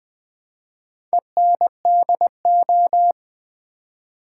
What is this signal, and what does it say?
Morse code, heard as a keyed tone, about 20 wpm, 700 Hz. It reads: ENDO
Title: Build A Word - 4 Letters At A Time - 11-Letter Words - 20wpm